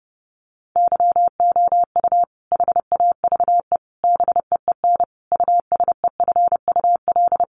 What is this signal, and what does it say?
Morse code, heard as a keyed tone, about 28 wpm, 700 Hz. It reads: YOU HAVE BEEN USEFUL
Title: Sentences from Top 700 Words 30wpm